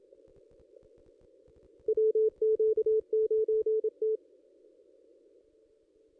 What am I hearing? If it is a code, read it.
Morse code, heard as a keyed tone, about 27 wpm, 440 Hz. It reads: WQ9T
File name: contests / ss-cw-k5tr-op-n5rz -1900